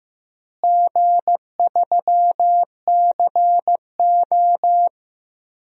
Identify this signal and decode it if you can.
Morse code, heard as a keyed tone, about 15 wpm, 700 Hz. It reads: G3CO
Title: DXCC-100 Call Signs - 4 Characters - Repeated 2x - 15wpm